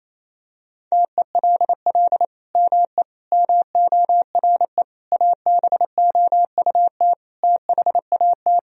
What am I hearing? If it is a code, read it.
Morse code, heard as a keyed tone, about 27 wpm, 700 Hz. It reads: TELL ME MORE ABOUT THAT
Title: Sentences from Top 1000 Words - Review Part 3 - 28wpm